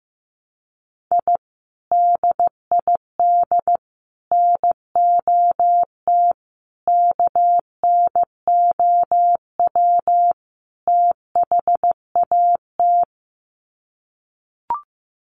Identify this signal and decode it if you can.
Morse code, heard as a keyed tone, about 15 wpm, 700 Hz. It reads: I DID NOT KNOW THAT
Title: Sets of 5 Words 15wpm